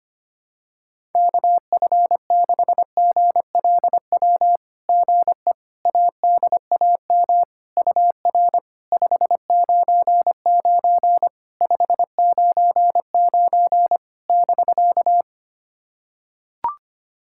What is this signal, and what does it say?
Morse code, heard as a keyed tone, about 25 wpm, 690 Hz. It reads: KF6GLW GE ADAM UR 599 599 <BK>
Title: SOTA QSO Practice 25wpm